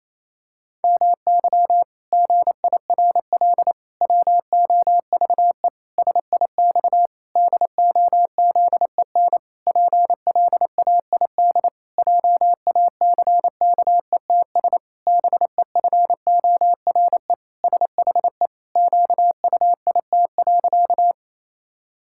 Morse code, heard as a keyed tone, about 28 wpm, 700 Hz. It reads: MY GIRL WOVE SIX DOZEN PLAID JACKETS BEFORE SHE QUIT.
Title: Pangram Challenge v2 - 28wpm